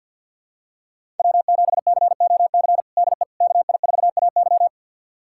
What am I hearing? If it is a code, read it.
Morse code, heard as a keyed tone, about 46 wpm, 700 Hz. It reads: W7COC DE KI4AY
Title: USA Callsign DE Callsign - 5 Characters - 50wpm